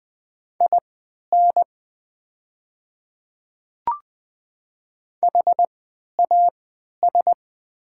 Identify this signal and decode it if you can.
Morse code, heard as a keyed tone, about 20 wpm, spaced as for 11 wpm, 700 Hz. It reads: IN HAS
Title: 3x Character Spacing - Top 100 Words - 20wpm